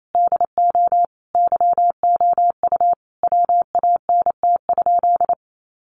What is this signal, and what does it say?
Morse code, heard as a keyed tone, about 28 wpm, 700 Hz. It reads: DO YOU WANT?
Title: Sentences from Top 200 Words 28wpm